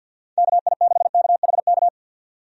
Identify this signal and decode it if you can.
Morse code, heard as a keyed tone, about 51 wpm, 700 Hz. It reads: KI6KHK
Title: USA Call Signs - 6 Characters - 50wpm